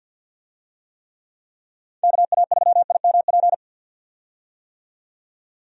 Morse code, heard as a keyed tone, about 51 wpm, 700 Hz. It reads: KA3IMP
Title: USA Call Signs - 6 Characters - 50wpm